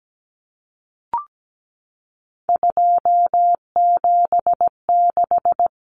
Morse code, heard as a keyed tone, about 17 wpm, 700 Hz. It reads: 276
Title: Numbers - 3-Digit 17wpm